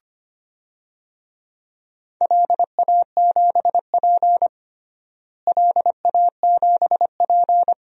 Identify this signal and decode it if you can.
Morse code, heard as a keyed tone, about 25 wpm, 700 Hz. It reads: LA7P LA7P
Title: DXCC-100 Call Signs - 4 Characters - Repeated 2x - 25wpm